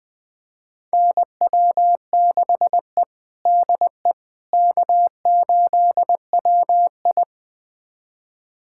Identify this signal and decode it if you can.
Morse code, heard as a keyed tone, about 20 wpm, 700 Hz. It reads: NW6E DE K8WI
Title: USA Callsign DE Callsign - 4 Characters - 20wpm